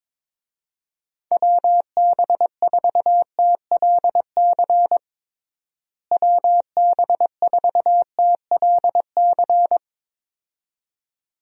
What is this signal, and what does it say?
Morse code, heard as a keyed tone, about 22 wpm, 700 Hz. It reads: WB4TLC WB4TLC
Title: USA Call Signs - 6 Characters - Repeated 2x - 22wpm